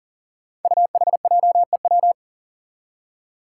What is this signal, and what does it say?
Morse code, heard as a keyed tone, about 40 wpm, 700 Hz. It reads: UHJEW
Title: Five Challenging Letter-Number Combinations 40wpm